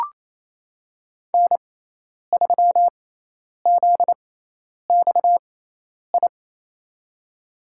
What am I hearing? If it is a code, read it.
Morse code, heard as a keyed tone, about 28 wpm, spaced as for 10 wpm, 700 Hz. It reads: N3ZXI
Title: USA Call Signs - 5 Characters - 6x Character Spacing - 28wpm